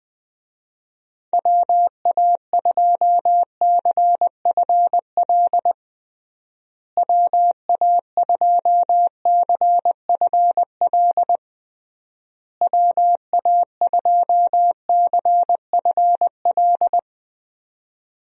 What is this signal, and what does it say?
Morse code, heard as a keyed tone, about 20 wpm, 700 Hz. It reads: WA2CFL WA2CFL WA2CFL
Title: USA Call Signs - 6 Characters - Repeated 3x - 20wpm